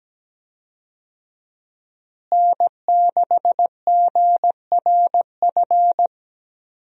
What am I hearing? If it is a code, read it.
Morse code, heard as a keyed tone, about 17 wpm, 700 Hz. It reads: N6GRF
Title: USA Call Signs - 5 Characters - Repeated 3x - 17wpm